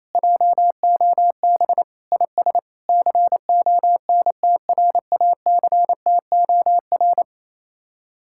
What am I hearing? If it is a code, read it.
Morse code, heard as a keyed tone, about 31 wpm, 700 Hz. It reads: JOB IS CONTRACTOR
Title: Speed-Racing Common QSO Phrases Encore - 28wpm